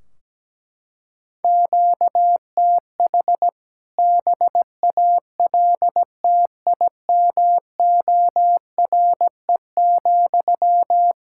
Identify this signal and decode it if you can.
Morse code, heard as a keyed tone, about 17 wpm, 700 Hz. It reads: QTH BALTIMORE,
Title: K1USN SST QSO Practice 17wpm